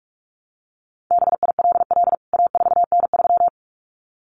Morse code, heard as a keyed tone, about 44 wpm, 700 Hz. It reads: BILL A4N3
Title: CWT Exchange 45wpm